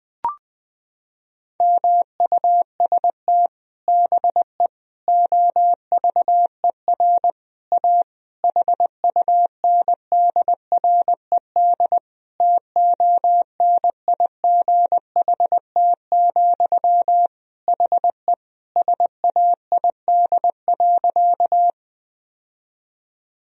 Morse code, heard as a keyed tone, about 21 wpm, 700 Hz. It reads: MUST BE OVER A HUNDRED TONIGHT, HE SAID.
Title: Sentences from Top 1000 Words - Review Part 2 - 20wpm